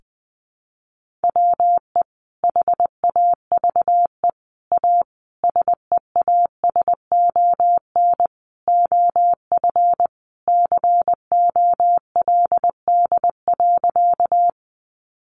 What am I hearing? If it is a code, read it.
Morse code, heard as a keyed tone, about 19 wpm, 700 Hz. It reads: WE HAVE A SEASON OF COLD.
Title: Sentences from Top 1400 Words 20wpm